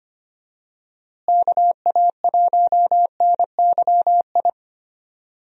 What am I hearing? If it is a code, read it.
Morse code, heard as a keyed tone, about 25 wpm, 700 Hz. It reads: KA1NYI
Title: USA Call Signs - 6 Characters - Repeated 2x - 25wpm